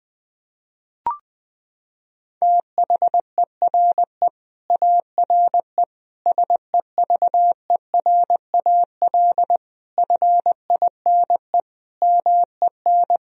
Morse code, heard as a keyed tone, about 20 wpm, 700 Hz. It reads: THERE ARE SEVERAL FINE MEN
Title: Sentences from Top 1000 Words - Review Part 2 - 20wpm